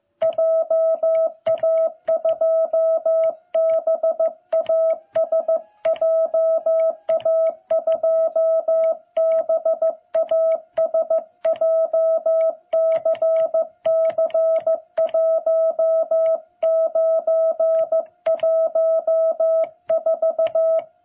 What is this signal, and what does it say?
Morse code, heard as a keyed tone, about 15 wpm, 660 Hz. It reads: JA2BAS JA2BAS JCC1914
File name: jh8cbh / 2021cw